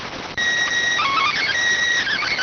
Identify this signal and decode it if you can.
SSTV (slow-scan television), PD 290.